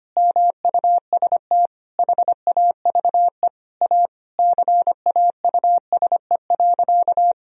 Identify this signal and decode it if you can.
Morse code, heard as a keyed tone, about 25 wpm, 700 Hz. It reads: MUST HAVE A CAUSE.
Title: Sentences from Top 1000 Words - Review Part 1 - 25wpm